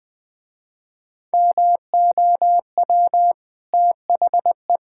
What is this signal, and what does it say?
Morse code, heard as a keyed tone, about 19 wpm, 700 Hz. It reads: MOW THE
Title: Easy Sentences with Hints - 20wpm